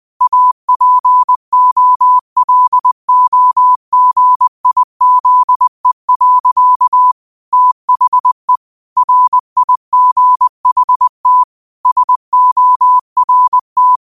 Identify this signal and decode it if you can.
Morse code, heard as a keyed tone, about 20 wpm, 1000 Hz. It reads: APOLOGIZE. THE RIGHT SORT
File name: qotdfiles / 20wpm